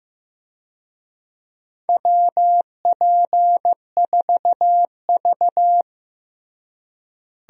Morse code, heard as a keyed tone, about 15 wpm, 700 Hz. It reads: WP4V
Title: USA Call Signs - 4 Characters - Repeated 3x - 15wpm